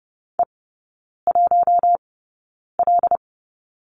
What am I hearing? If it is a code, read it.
Morse code, heard as a keyed tone, about 30 wpm, spaced as for 9 wpm, 700 Hz. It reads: E1L
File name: 081 Lesson 14e - Number 2 - Three Characters 30wpm